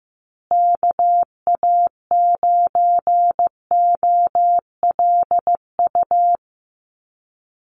Morse code, heard as a keyed tone, about 15 wpm, 700 Hz. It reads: KA9OLU
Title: USA Call Signs Repeated 2x 15wpm